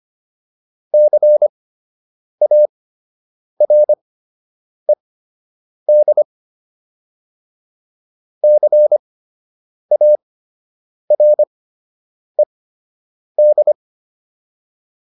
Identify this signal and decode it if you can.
Morse code, heard as a keyed tone, about 25 wpm, spaced as for 8 wpm, 600 Hz. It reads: CARED CARED